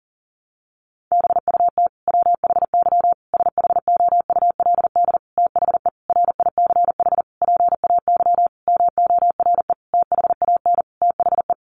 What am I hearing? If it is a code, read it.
Morse code, heard as a keyed tone, about 40 wpm, 700 Hz. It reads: BUT WHY SHOULD THE RICH PAY MORE THAN THE
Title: Sentences from Top 1100 Words 40wpm